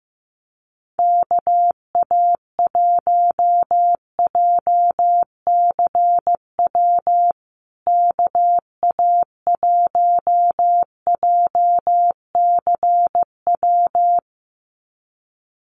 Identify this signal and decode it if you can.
Morse code, heard as a keyed tone, about 15 wpm, 700 Hz. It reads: KA1JCW KA1JCW
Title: USA Call Signs Repeated 2x 15wpm